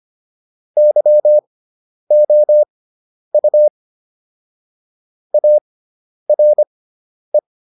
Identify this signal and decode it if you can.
Morse code, heard as a keyed tone, about 25 wpm, spaced as for 10 wpm, 600 Hz. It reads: YOU ARE